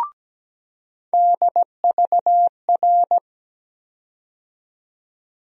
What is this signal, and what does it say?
Morse code, heard as a keyed tone, about 17 wpm, 700 Hz. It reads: DVR
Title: USA Call Sign Postfixes - 3 Letters 17wpm